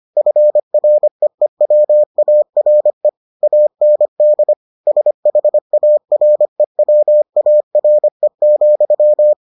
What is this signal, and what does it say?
Morse code, heard as a keyed tone, about 25 wpm, 600 Hz. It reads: FREEWARE AND SHAREWARE,